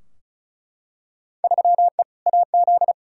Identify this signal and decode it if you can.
Morse code, heard as a keyed tone, about 36 wpm, 700 Hz. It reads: 3E AZ
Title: Field Day Exchanges Encore - 35wpm